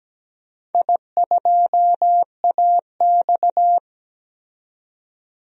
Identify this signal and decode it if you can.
Morse code, heard as a keyed tone, about 17 wpm, 700 Hz. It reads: I2AX
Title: DXCC-100 Call Signs - 4 Characters - 17wpm